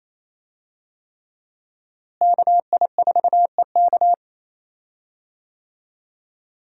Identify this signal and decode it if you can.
Morse code, heard as a keyed tone, about 28 wpm, 700 Hz. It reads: KI4EK
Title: USA Call Signs - 5 Characters - Repeated 3x - 28wpm